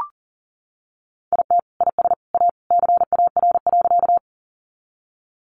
Morse code, heard as a keyed tone, about 40 wpm, 700 Hz. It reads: IT IS A CAR.
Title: Easy Phrases 40wpm